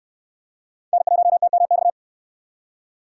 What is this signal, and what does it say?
Morse code, heard as a keyed tone, about 65 wpm, 700 Hz. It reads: N1TMX